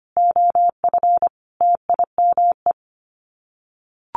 Morse code, heard as a keyed tone, about 25 wpm, 700 Hz. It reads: OF TIME
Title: Sets of 5 Words Encore2 25wpm